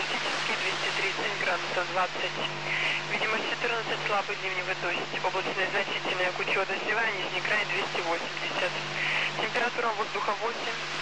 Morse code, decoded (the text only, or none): none